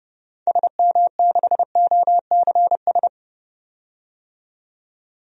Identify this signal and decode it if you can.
Morse code, heard as a keyed tone, about 30 wpm, 700 Hz. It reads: SM6OCS
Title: DXCC-100 Call Signs - 6 Characters - 30wpm